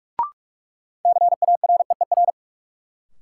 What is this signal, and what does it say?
Morse code, heard as a keyed tone, about 42 wpm, 700 Hz. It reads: CAREER
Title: Sentences from Top 1000 Words - New Word Drill - 45wpm